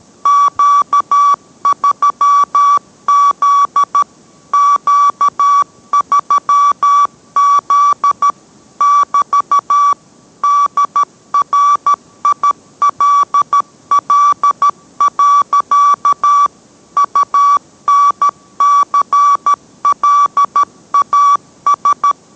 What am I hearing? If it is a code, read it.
Morse code, heard as a keyed tone, about 13 wpm, 1150 Hz. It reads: Q3Z Q3Z <BT> DRILL. UNCLAS